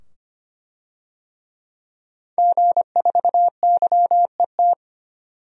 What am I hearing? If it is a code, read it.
Morse code, heard as a keyed tone, about 25 wpm, 700 Hz. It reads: G4YET
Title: DXCC-100 Call Signs - 5 Characters - 25wpm